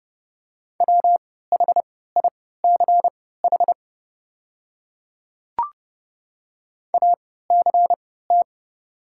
Morse code, heard as a keyed tone, about 30 wpm, spaced as for 17 wpm, 700 Hz. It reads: WHICH ACT
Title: 3x Character Spacing - Top 200 Words - 30wpm